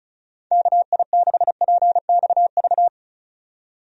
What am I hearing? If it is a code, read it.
Morse code, heard as a keyed tone, about 34 wpm, 700 Hz. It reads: KI6PXV
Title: USA Call Signs - 6 Characters - 35wpm